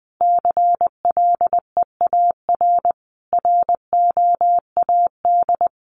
Morse code, heard as a keyed tone, about 20 wpm, 700 Hz. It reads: CLEAR ROAD
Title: Sentences from Top 800 Words 20wpm